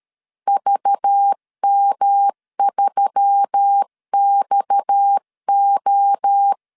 Morse code, heard as a keyed tone, about 13 wpm, 800 Hz. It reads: VM3XO
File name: LEZIONI / 9 286 12wpm 2305